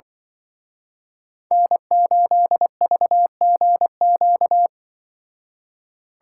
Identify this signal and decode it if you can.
Morse code, heard as a keyed tone, about 24 wpm, 700 Hz. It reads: N8VGQ
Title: Speed Racing - USA Call Signs 20wpm